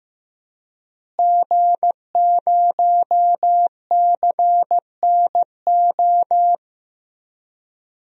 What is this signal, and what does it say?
Morse code, heard as a keyed tone, about 15 wpm, 700 Hz. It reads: G0CNO
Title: DXCC-100 Call Signs - 5 Characters - Repeated 3x - 15wpm